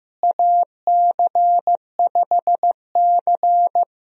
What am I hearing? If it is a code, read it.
Morse code, heard as a keyed tone, about 15 wpm, 700 Hz. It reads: AC5C
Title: USA Call Signs - 4 Characters - Repeated 2x - 15wpm